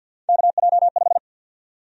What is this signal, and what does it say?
Morse code, heard as a keyed tone, about 46 wpm, 700 Hz. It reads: KJ5
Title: USA Call Sign Prefixes - 50wpm